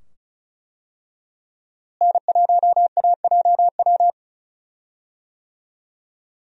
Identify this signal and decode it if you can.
Morse code, heard as a keyed tone, about 34 wpm, 700 Hz. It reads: N1AJW
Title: USA Call Signs - 5 Characters - Repeated 2x - 35wpm